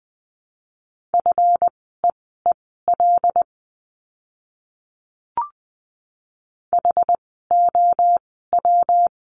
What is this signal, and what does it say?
Morse code, heard as a keyed tone, about 20 wpm, spaced as for 14 wpm, 700 Hz. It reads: FEEL HOW
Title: 206 Lesson 33f - Letter j - Words 20wpm